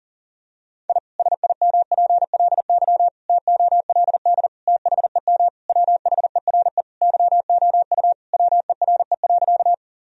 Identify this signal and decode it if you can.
Morse code, heard as a keyed tone, about 40 wpm, 700 Hz. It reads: I SIMPLY TOLD THEM WHERE YOU WERE.
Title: Sentences from Top 1000 Words - Review Part 4 - 40wpm